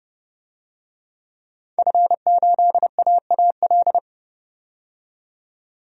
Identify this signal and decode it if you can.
Morse code, heard as a keyed tone, about 30 wpm, 700 Hz. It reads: F8AAL
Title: DXCC-100 Call Signs - 5 Characters - 30wpm